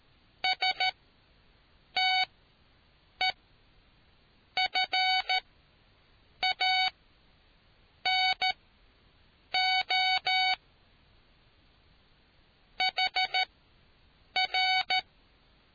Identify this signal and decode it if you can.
Morse code, heard as a keyed tone, about 13 wpm, spaced as for 6 wpm, 750 Hz. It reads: STEFANO HR